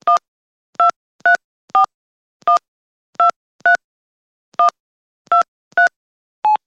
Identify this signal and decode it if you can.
DTMF, heard as telephone touch tones, tones of about 130 ms. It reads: 1234123123
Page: Медленно набираем номер на мобильном телефоне